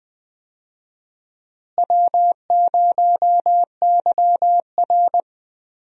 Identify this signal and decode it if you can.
Morse code, heard as a keyed tone, about 20 wpm, 700 Hz. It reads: W0YR
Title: USA Call Signs - 4 Characters - Repeated 2x - 20wpm